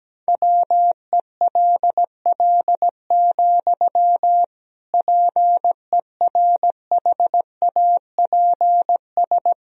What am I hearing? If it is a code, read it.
Morse code, heard as a keyed tone, about 17 wpm, 700 Hz. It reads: WELL, PERHAPS
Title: Sentences from Top 1000 Words - Review Part 2 - 17wpm